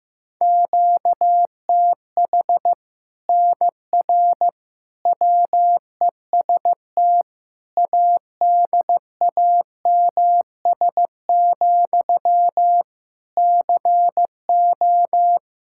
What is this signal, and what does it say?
Morse code, heard as a keyed tone, about 15 wpm, 700 Hz. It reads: QTH NR WEST ADAMS, CO
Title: Practice QSO - Exchange 1 - Beginning - 15wpm